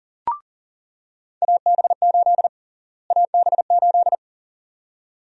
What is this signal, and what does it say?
Morse code, heard as a keyed tone, about 40 wpm, 700 Hz. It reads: AB8 AB8
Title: USA Call Sign Prefixes - Repeated 2x - 40wpm